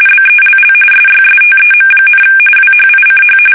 RTTY, about 45.45 baud. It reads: BELL SYSTEM, WHICH
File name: fsk850 / teletype-corp